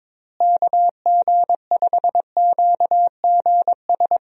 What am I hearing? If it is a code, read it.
Morse code, heard as a keyed tone, about 21 wpm, 700 Hz. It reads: KG5QGS
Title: USA Call Signs - 6 Characters - Repeated 3x - 22wpm